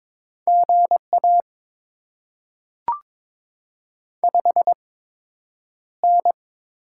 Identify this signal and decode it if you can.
Morse code, heard as a keyed tone, about 23 wpm, spaced as for 10 wpm, 700 Hz. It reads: GA 5N
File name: Top 20 QSO Elements - 8x Character Spacing - 22wpm